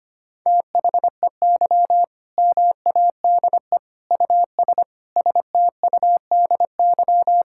Morse code, heard as a keyed tone, about 25 wpm, 700 Hz. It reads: THEY MADE US STUDY
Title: Sentences from Top 1000 Words - Review Part 6 - 25wpm